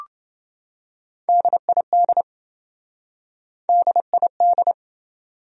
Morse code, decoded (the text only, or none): DID DID